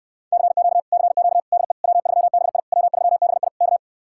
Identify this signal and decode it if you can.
Morse code, heard as a keyed tone, about 69 wpm, 700 Hz. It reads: CQ CQ DE W3BI W3BI K